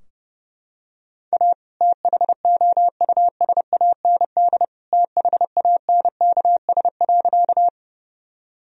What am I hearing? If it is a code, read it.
Morse code, heard as a keyed tone, about 32 wpm, 700 Hz. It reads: A THOUSAND THANKS.
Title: Sentences from Top 1000 Words - Review Part 3 - 30wpm